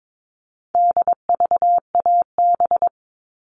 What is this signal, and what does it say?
Morse code, heard as a keyed tone, about 22 wpm, 700 Hz. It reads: DVAB
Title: All Cut Numbers - 4-Digit 22wpm